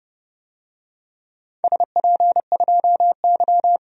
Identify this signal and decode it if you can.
Morse code, heard as a keyed tone, about 31 wpm, 700 Hz. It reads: SP2Y
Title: DXCC-100 Call Signs - 4 Characters - 30wpm